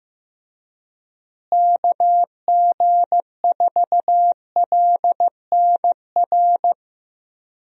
Morse code, heard as a keyed tone, about 15 wpm, 700 Hz. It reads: KG4LNR
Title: USA Call Signs - 6 Characters - Repeated 3x - 15wpm